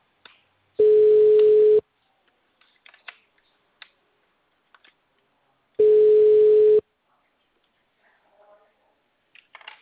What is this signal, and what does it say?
Ringback tone (Europe, 425 Hz).